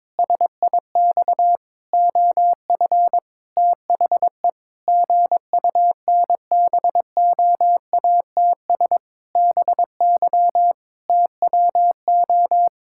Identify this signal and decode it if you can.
Morse code, heard as a keyed tone, about 22 wpm, 700 Hz. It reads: SIX OF THE GUNBOATS BY TWO
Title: Pangram Challenge v2 - 22wpm